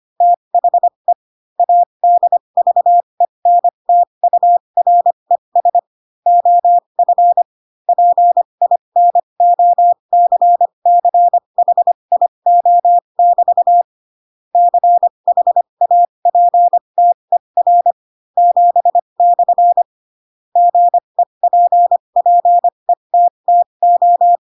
Morse code, heard as a keyed tone, about 25 wpm, spaced as for 22 wpm, 700 Hz. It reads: THE ADVENTURES OF PINOCCHIO<BT> CHAPTER 7/ GEPPETTO
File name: Pinocchio / Pinocchio ch7 18 25 ik1hge